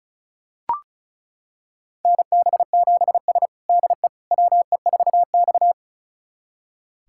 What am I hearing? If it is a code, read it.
Morse code, heard as a keyed tone, about 34 wpm, 700 Hz. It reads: NB7S DE WE4X
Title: USA Callsign DE Callsign - 4 Characters - 35wpm